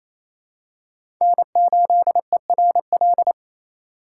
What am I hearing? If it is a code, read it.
Morse code, heard as a keyed tone, about 27 wpm, 700 Hz. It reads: N8ERL
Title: USA Call Signs - 5 Characters - 4x Character Spacing - 28wpm